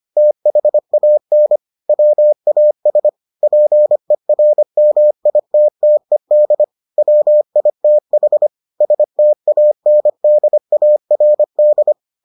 Morse code, heard as a keyed tone, about 25 wpm, 600 Hz. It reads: THAN WAS PERMITTED WITH STANDARD